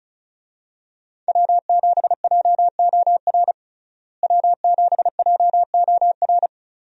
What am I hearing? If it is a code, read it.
Morse code, heard as a keyed tone, about 35 wpm, 700 Hz. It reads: W7JOR W7JOR
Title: USA Call Signs - 5 Characters - Repeated 2x - 35wpm